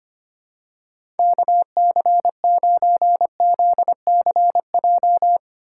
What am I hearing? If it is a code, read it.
Morse code, heard as a keyed tone, about 25 wpm, 700 Hz. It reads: KC9ZCJ
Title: USA Call Signs - 6 Characters - 8x Character Spacing - 25wpm